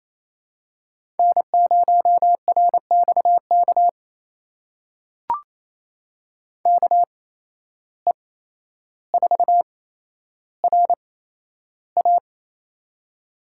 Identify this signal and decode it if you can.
Morse code, heard as a keyed tone, about 28 wpm, 700 Hz. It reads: N 0 R X K K E 4 R A
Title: USA Call Signs - 5 Characters - 8x Character Spacing - 28wpm